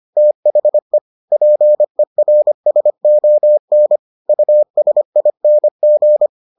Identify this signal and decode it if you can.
Morse code, heard as a keyed tone, about 25 wpm, 600 Hz. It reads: THE PERSON USING